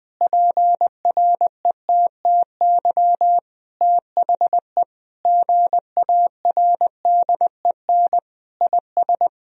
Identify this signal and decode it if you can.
Morse code, heard as a keyed tone, about 20 wpm, 700 Hz. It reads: PRETTY THE GARDEN IS
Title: Sentences from Top 1000 Words - Review Part 4 - 20wpm